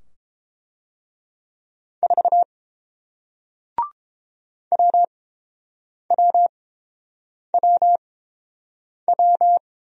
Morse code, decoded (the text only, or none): 4 WWWW